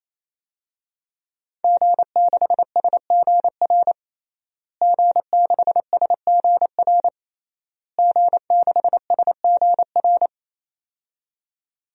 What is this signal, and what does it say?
Morse code, heard as a keyed tone, about 28 wpm, 700 Hz. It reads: G6SGR G6SGR G6SGR
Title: DXCC-100 Call Signs - 5 Characters - Repeated 3x - 28wpm